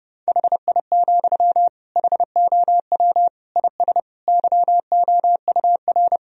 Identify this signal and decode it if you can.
Morse code, heard as a keyed tone, about 30 wpm, 700 Hz. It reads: HI, HOW IS YOUR
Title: Sentences from Top 1000 Words - Review Part 3 - 30wpm